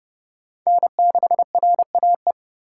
Morse code, heard as a keyed tone, about 29 wpm, 700 Hz. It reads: N6RAE